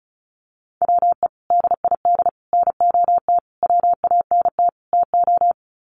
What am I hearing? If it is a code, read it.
Morse code, heard as a keyed tone, about 34 wpm, 700 Hz. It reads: WE DID NOT WANT TO
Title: Sets of 5 Words Encore 35wpm